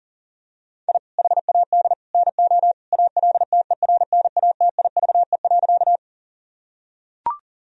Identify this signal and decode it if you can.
Morse code, heard as a keyed tone, about 40 wpm, 700 Hz. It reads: I HAD NO ALTERNATIVE.
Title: Sentences from Top 1000 Words - Review Part 4 - 40wpm